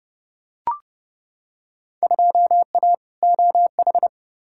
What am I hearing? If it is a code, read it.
Morse code, heard as a keyed tone, about 30 wpm, 700 Hz. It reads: 2A OH
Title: Field Day Exchanges - 30wpm